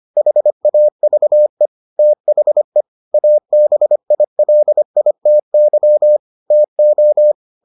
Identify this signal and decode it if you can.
Morse code, heard as a keyed tone, about 25 wpm, 600 Hz. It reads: HAVE THE ABILITY TO